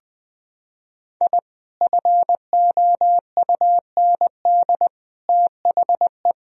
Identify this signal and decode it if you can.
Morse code, heard as a keyed tone, about 20 wpm, 700 Hz. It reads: I FOUND THE
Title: Sentences from Top 1000 Words - Review Part 3 - 20wpm